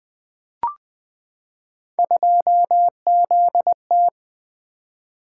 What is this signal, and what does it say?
Morse code, heard as a keyed tone, about 20 wpm, 700 Hz. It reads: 2ZT
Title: Three Letter-Number Combinations - 20wpm